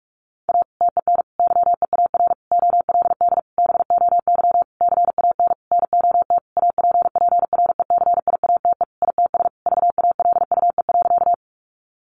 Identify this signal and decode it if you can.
Morse code, heard as a keyed tone, about 44 wpm, 700 Hz. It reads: A TEN YEAR OLD BOY CAN NOT APPRECIATE ITS VALUE.
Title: Sentences from Top 1300 Words 45wpm